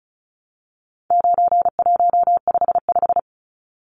Morse code, heard as a keyed tone, about 34 wpm, 700 Hz. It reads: 9155
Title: Numbers - 4-Digit 35wpm